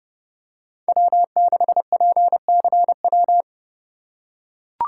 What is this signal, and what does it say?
Morse code, heard as a keyed tone, about 30 wpm, 700 Hz. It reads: W6PCW
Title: USA Call Signs - 5 Characters - 4x Character Spacing - 30wpm